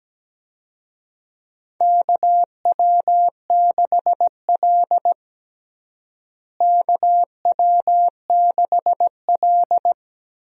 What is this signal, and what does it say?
Morse code, heard as a keyed tone, about 17 wpm, 700 Hz. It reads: KW6L KW6L
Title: USA Call Signs - 4 Characters - Repeated 2x - 17wpm